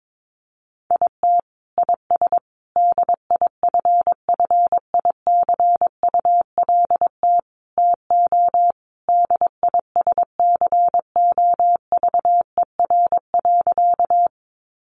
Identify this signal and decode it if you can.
Morse code, heard as a keyed tone, about 22 wpm, 700 Hz. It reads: IT IS DIFFICULT TO DISCOVER.
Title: Sentences from Top 1100 Words 22wpm